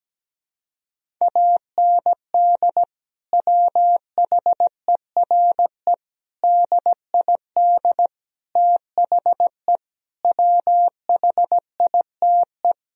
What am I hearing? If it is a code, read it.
Morse code, heard as a keyed tone, about 17 wpm, 700 Hz. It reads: AND WHERE DID THE WHITE
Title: Sentences from Top 1000 Words - Review Part 1 - 17wpm